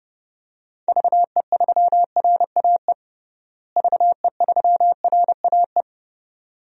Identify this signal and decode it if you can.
Morse code, heard as a keyed tone, about 30 wpm, 700 Hz. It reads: VE3RAE VE3RAE
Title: DXCC-100 Call Signs - 6 Characters - Repeated 2x - 30wpm